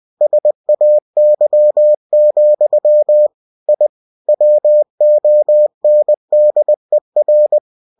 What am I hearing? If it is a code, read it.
Morse code, heard as a keyed tone, about 20 wpm, 600 Hz. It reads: SAY, I WONDER
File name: cw-books / winnie-the-pooh / winnie-20wpm-page-0110